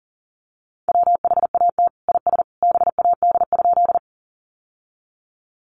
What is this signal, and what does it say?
Morse code, heard as a keyed tone, about 40 wpm, 700 Hz. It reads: WHAT IS BAD?
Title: Sentences from Top 700 Words 40wpm